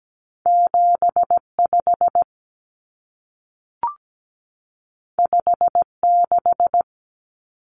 Morse code, heard as a keyed tone, about 17 wpm, 700 Hz. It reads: 75 56
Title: Common Cut Numbers - 2-Digit 17wpm